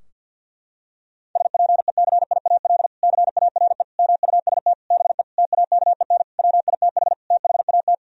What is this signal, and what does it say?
Morse code, heard as a keyed tone, about 50 wpm, 700 Hz. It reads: SPECIAL CARE MUST BE TAKEN WITH THAT
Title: Sentences from Top 1000 Words - Review Part 4 - 50wpm